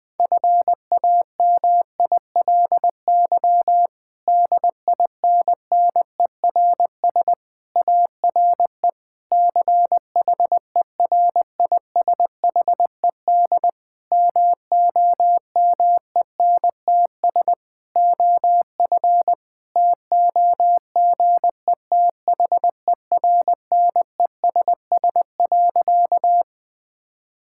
Morse code, heard as a keyed tone, about 20 wpm, 700 Hz. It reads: FAMILY DINNERS ARE CHERISHED MOMENTS OF TOGETHERNESS.
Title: Difficult Sentences with Hints - 20wpm